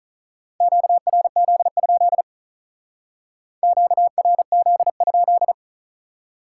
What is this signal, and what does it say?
Morse code, heard as a keyed tone, about 46 wpm, 700 Hz. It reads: QRZ? QRZ?